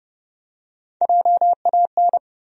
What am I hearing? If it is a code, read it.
Morse code, heard as a keyed tone, about 32 wpm, 700 Hz. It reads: JAN
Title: USA Common Names - 30wpm